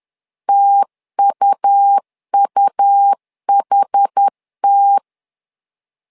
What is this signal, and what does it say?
Morse code, heard as a keyed tone, about 10 wpm, 800 Hz. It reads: TUUHT
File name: LEZIONI / 2 UHDN / 2 UHD 10wpm 1826